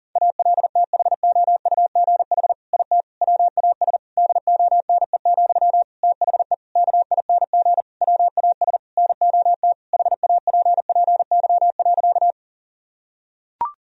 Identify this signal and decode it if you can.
Morse code, heard as a keyed tone, about 40 wpm, 700 Hz. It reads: ALTHOUGH IT WAS DONE, THE KING WAS NOT HAPPY.
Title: Sentences from Top 1000 Words - Review Part 3 - 40wpm